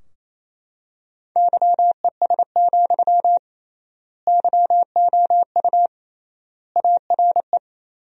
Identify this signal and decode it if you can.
Morse code, heard as a keyed tone, about 28 wpm, 700 Hz. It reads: YES, YOU ARE
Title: 3x Word-Spacing - Sentences from Top 500 Words - 28wpm